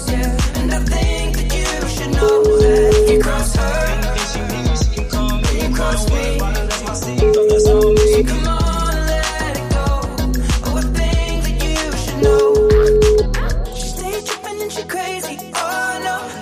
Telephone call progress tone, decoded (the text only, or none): ringback tone